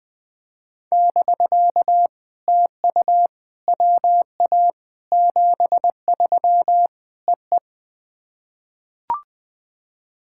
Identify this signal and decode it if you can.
Morse code, heard as a keyed tone, about 21 wpm, 700 Hz. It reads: <BK> TU WA 73 EE
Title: POTA QSO Practice - 20wpm